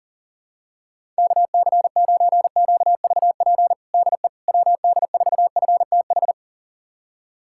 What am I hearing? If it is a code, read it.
Morse code, heard as a keyed tone, about 37 wpm, 700 Hz. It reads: KC9QVP DE WD4FTH
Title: USA Callsign DE Callsign - 6 Characters - 40wpm